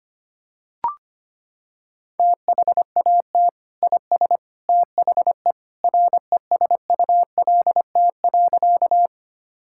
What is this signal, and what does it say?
Morse code, heard as a keyed tone, about 25 wpm, 700 Hz. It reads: THAT IS THE RESULT.
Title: Sentences from Top 1000 Words - Review Part 5 - 25wpm